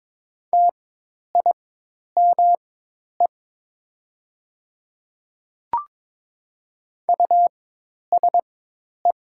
Morse code, heard as a keyed tone, about 22 wpm, spaced as for 10 wpm, 700 Hz. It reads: TIME USE
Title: 4x Character Spacing - Top 200 Words - 22wpm